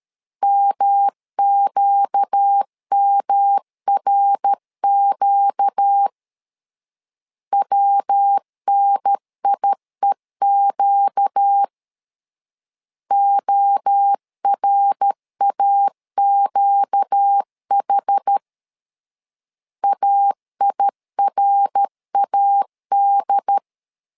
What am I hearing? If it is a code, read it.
Morse code, heard as a keyed tone, about 13 wpm, 800 Hz. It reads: MQMRQ WNIEQ ORAQH AIRAD
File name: LEZIONI / 4 Q 12wpm 1833